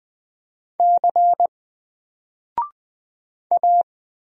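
Morse code, heard as a keyed tone, about 20 wpm, 700 Hz. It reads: C A
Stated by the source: Single letter and number Morse drill at 20 wpm, from N4LCD.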